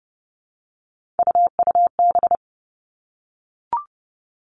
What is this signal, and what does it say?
Morse code, heard as a keyed tone, about 30 wpm, 700 Hz. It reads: UUB
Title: Three Challenging Letter-Number Combinations 30wpm